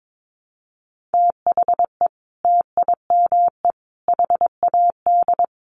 Morse code, heard as a keyed tone, about 22 wpm, 700 Hz. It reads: THE TIME HAD
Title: Sentences from Top 100 Words 22wpm